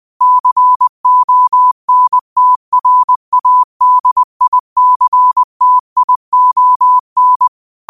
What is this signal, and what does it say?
Morse code, heard as a keyed tone, about 20 wpm, 1000 Hz. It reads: CONTRADICTION
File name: qotdfiles / 20wpm